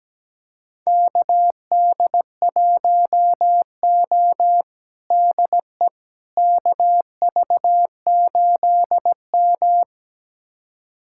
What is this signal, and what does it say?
Morse code, heard as a keyed tone, about 17 wpm, 700 Hz. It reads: KD1O DE KV8M
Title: USA Callsign DE Callsign - 4 Characters - 17wpm